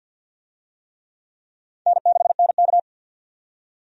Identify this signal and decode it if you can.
Morse code, heard as a keyed tone, about 47 wpm, 700 Hz. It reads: N6NK
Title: USA Call Signs - 4 Characters - 50wpm